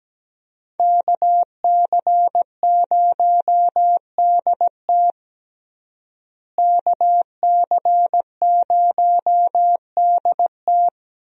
Morse code, heard as a keyed tone, about 17 wpm, 700 Hz. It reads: KC0DT KC0DT
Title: USA Call Signs - 5 Characters - Repeated 3x - 17wpm